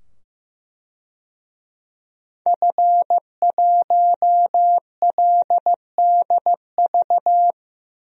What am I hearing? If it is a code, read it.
Morse code, heard as a keyed tone, about 15 wpm, 700 Hz. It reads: F1LDV
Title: DXCC-100 Call Signs - 5 Characters - 15wpm